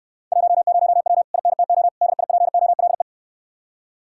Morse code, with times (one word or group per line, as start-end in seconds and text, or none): 0.3-1.2 10W
1.3-1.9 INTO
2.0-3.0 DIPOLE